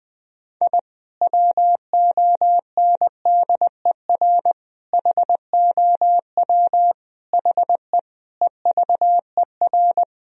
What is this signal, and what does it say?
Morse code, heard as a keyed tone, about 20 wpm, 700 Hz. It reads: I WONDER HOW HE EVER
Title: Sentences from Top 1000 Words - Review Part 3 - 20wpm